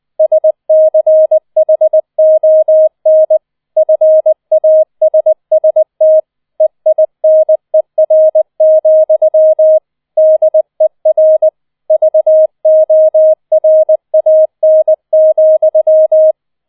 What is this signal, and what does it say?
Morse code, heard as a keyed tone, about 20 wpm, 630 Hz. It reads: SCHON FASST EINER, DER VORAN,
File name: 20wpm